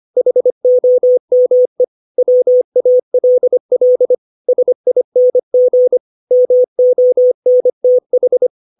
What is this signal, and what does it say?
Morse code, heard as a keyed tone, about 25 wpm, 500 Hz. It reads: HOME WALL SING MONTH
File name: 632 words / 632 Words 25WPM